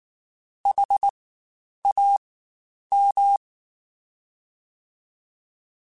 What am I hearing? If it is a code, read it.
Morse code, heard as a keyed tone, about 19 wpm, spaced as for 9 wpm, 800 Hz. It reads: HAM
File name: Spaced Alphas / Spaced Alphas mp3s / Hs sp @18